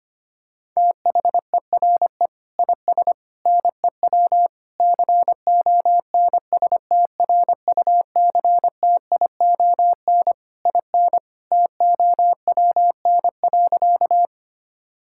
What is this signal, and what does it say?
Morse code, heard as a keyed tone, about 25 wpm, 700 Hz. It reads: THERE IS NEW CONSTRUCTION IN TOWN.
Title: Sentences from Top 1600 Words 25wpm